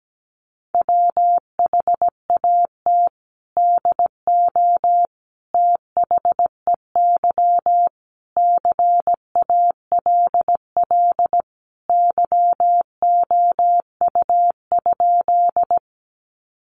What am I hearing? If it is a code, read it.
Morse code, heard as a keyed tone, about 17 wpm, 700 Hz. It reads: WHAT DO THEY CALL YOU?
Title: Sentences from Top 100 Words 17wpm